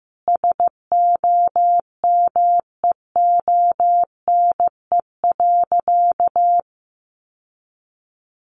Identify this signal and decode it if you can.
Morse code, heard as a keyed tone, about 15 wpm, 700 Hz. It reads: SOMEONE.
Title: Sentences from Top 1300 Words 15wpm